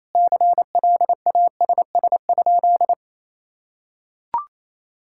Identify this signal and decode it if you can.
Morse code, heard as a keyed tone, about 28 wpm, 700 Hz. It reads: CLASS?
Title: Sentences from Top 1000 Words - Review Part 6 - 28wpm